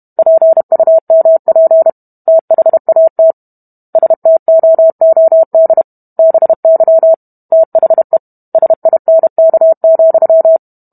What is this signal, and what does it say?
Morse code, heard as a keyed tone, about 32 wpm, 650 Hz. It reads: PUMP THAT STOOD BY THE SINK,